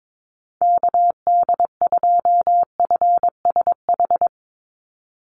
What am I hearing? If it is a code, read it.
Morse code, heard as a keyed tone, about 22 wpm, 690 Hz. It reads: KD2FSH
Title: Call Signs Repeated 3x - 22wpm